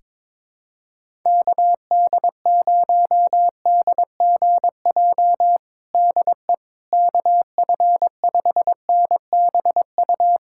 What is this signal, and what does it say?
Morse code, heard as a keyed tone, about 22 wpm, 700 Hz. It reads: KD0DGJ DE KF5NBU
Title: USA Callsign DE Callsign - 6 Characters - 22wpm